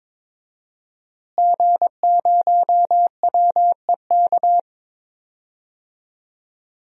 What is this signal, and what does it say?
Morse code, heard as a keyed tone, about 22 wpm, 700 Hz. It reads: G0WEK
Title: DXCC-100 Call Signs - 5 Characters - Repeated 2x - 22wpm